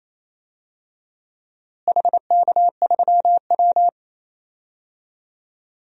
Morse code, heard as a keyed tone, about 28 wpm, 700 Hz. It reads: HK3W
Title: DXCC-100 Call Signs - 4 Characters - Repeated 3x - 28wpm